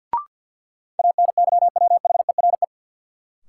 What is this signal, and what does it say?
Morse code, heard as a keyed tone, about 47 wpm, 700 Hz. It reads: ANYWHERE
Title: Sentences from Top 1400 Words - New Word Drill - 50wpm